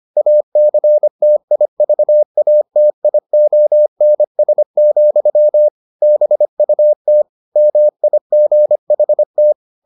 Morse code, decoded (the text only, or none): ACTIVATIONS, BUT MIGHT